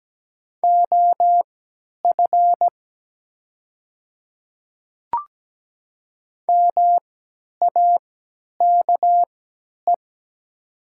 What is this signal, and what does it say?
Morse code, heard as a keyed tone, about 17 wpm, spaced as for 10 wpm, 700 Hz. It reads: OF MAKE
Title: 3x Character Spacing - Top 500 Words - 17wpm